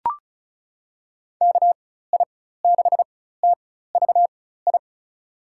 Morse code, heard as a keyed tone, about 34 wpm, spaced as for 16 wpm, 700 Hz. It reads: KI6TVI
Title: USA Call Signs - 6 Characters - 4x Character Spacing - 35wpm